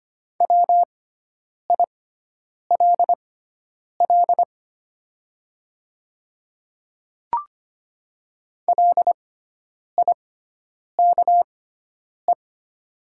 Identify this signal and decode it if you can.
Morse code, heard as a keyed tone, about 25 wpm, spaced as for 9 wpm, 700 Hz. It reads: WILL LIKE
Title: 6x Character Spacing - Top 100 Words - 25wpm